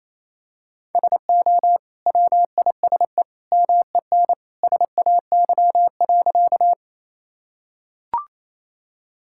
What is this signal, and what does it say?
Morse code, heard as a keyed tone, about 28 wpm, 700 Hz. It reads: SO WISE MEN SAY.
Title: Sentences from Top 1600 Words 28wpm